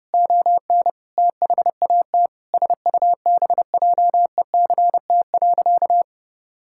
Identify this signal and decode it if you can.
Morse code, heard as a keyed tone, about 32 wpm, 700 Hz. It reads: ON THAT SUBJECT.
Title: Sentences from Top 1600 Words 30wpm